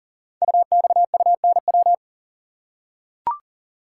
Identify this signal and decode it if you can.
Morse code, heard as a keyed tone, about 40 wpm, 700 Hz. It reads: UXUNW
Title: Code Groups - 5 Letters - 40wpm